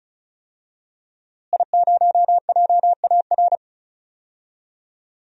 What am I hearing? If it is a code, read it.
Morse code, heard as a keyed tone, about 36 wpm, 700 Hz. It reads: I0JAR
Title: DXCC-100 Call Signs - 5 Characters - Repeated 3x - 35wpm